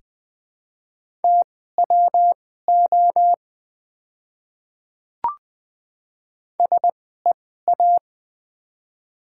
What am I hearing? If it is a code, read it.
Morse code, heard as a keyed tone, about 20 wpm, spaced as for 14 wpm, 700 Hz. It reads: TWO SEA
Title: 2x Character Spacing - Top 300 Words - 20wpm